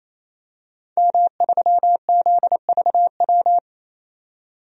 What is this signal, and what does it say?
Morse code, heard as a keyed tone, about 28 wpm, 700 Hz. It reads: M3ZVW
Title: DXCC-100 Call Signs - 5 Characters - 28wpm